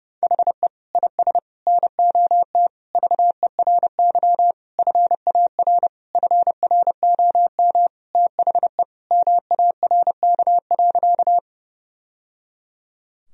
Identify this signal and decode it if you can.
Morse code, heard as a keyed tone, about 30 wpm, 700 Hz. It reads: HE IS NOT VERY FAR FROM THE MARK.
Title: Sentences from Top 1000 Words - Review Part 1 - 30wpm